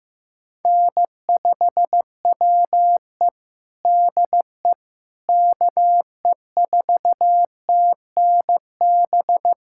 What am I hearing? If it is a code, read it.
Morse code, heard as a keyed tone, about 16 wpm, 700 Hz. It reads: N5WE DE KE4TNB
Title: K1USN SST QSO Practice 15wpm